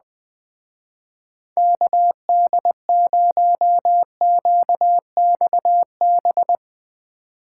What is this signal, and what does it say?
Morse code, heard as a keyed tone, about 22 wpm, 700 Hz. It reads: KD0QXB
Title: Speed Racing - USA 2x3 Call Signs 20wpm